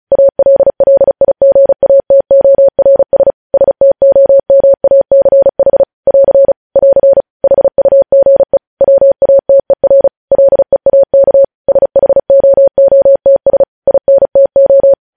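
Morse code, heard as a keyed tone, about 36 wpm, 550 Hz. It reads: ALLIGATORS STOMACH <AR> <AR> HUGE WATER LEAK SHOOTS INTO
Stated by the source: Morse code news headlines sent at 35, beginning with vvv.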